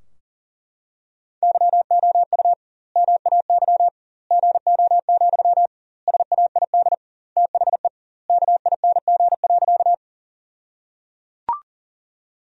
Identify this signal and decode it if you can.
Morse code, heard as a keyed tone, about 37 wpm, 700 Hz. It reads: YOU MAY GO, SAID THE KING.
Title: 2x Word-Spacing - Sentences from Top 400 Words - 40wpm